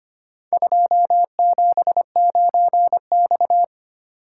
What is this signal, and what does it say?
Morse code, heard as a keyed tone, about 25 wpm, 700 Hz. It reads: 279X
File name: practice / 25wpm / Four Letter-Number Combinations - 25wpm